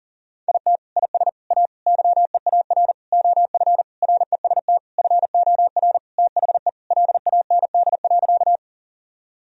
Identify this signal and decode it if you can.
Morse code, heard as a keyed tone, about 37 wpm, 700 Hz. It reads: IT IS A YEAR OF REST FOR THE LAND.
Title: Sentences from Top 1000 Words - Review Part 1 - 40wpm